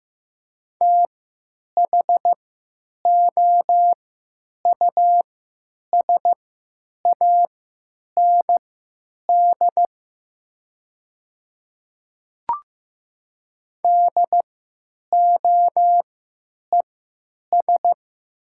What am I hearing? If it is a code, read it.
Morse code, heard as a keyed tone, about 15 wpm, spaced as for 9 wpm, 700 Hz. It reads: THOUSAND DOES
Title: 3x Character Spacing - Top 500 Words - 15wpm